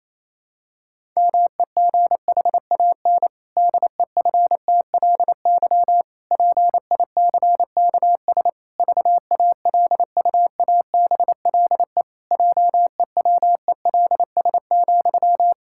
Morse code, heard as a keyed tone, about 28 wpm, 700 Hz. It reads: MEGHAN DEFTLY PICKS VALUABLE JEWELS,
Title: Pangram Challenge v2 - 28wpm